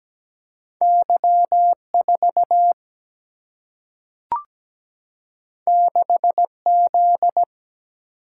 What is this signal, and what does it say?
Morse code, heard as a keyed tone, about 17 wpm, 700 Hz. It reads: Y4 6Z
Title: Single-Letter Single-Number 17wpm